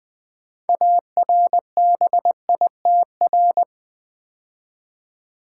Morse code, heard as a keyed tone, about 20 wpm, 700 Hz. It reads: ARBITR
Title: Build A Word - 3 Letters At A Time - 9-Letter Words - 20wpm